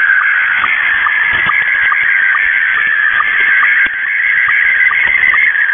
SSTV (slow-scan television), Scottie 1.
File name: mw / 0256 6950u wolverine radio inc sstv